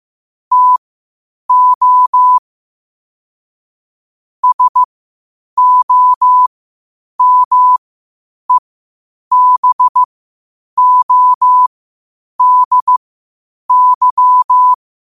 Morse code, heard as a keyed tone, about 15 wpm, spaced as for 8 wpm, 1000 Hz. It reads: TO SOMEBODY
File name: 8wpm